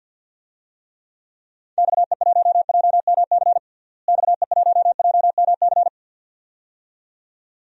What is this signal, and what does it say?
Morse code, heard as a keyed tone, about 47 wpm, 700 Hz. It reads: XE1JMC XE1JMC
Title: DXCC-100 Call Signs - 6 Characters - Repeated 2x - 50wpm